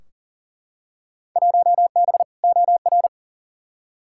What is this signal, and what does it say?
Morse code, heard as a keyed tone, about 40 wpm, 700 Hz. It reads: 1B OR
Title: Field Day Exchanges Encore - 40wpm